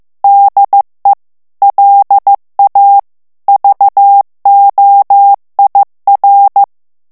Morse code, heard as a keyed tone, about 15 wpm, 800 Hz. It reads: DE LA VOIR
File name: FRANCAIS TEXTE CLAIR 2 VITESSE 14 MOTS MINUTES